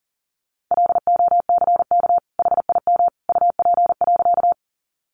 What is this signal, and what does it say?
Morse code, heard as a keyed tone, about 40 wpm, 700 Hz. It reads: LOCK HIM UP.
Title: Sentences from Top 1300 Words 40wpm